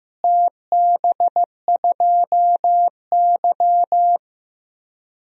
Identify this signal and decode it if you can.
Morse code, heard as a keyed tone, about 15 wpm, 700 Hz. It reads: TB2Y
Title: DXCC-100 Call Signs - 4 Characters - Repeated 2x - 15wpm